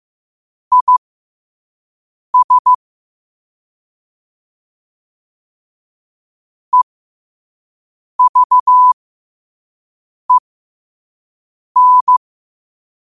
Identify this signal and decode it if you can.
Morse code, heard as a keyed tone, about 15 wpm, spaced as for 5 wpm, 1000 Hz. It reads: IS EVEN
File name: qotdfiles / 5wpm